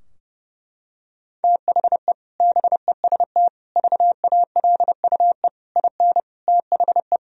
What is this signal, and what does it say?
Morse code, heard as a keyed tone, about 30 wpm, 700 Hz. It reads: THE BEST VALUE IN THE
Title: Sentences from Top 1000 Words - Review Part 3 - 30wpm